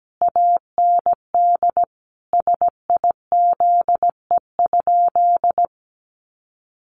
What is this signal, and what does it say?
Morse code, heard as a keyed tone, about 17 wpm, 700 Hz. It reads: AND SIZE?
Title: Sentences from Top 500 Words 17wpm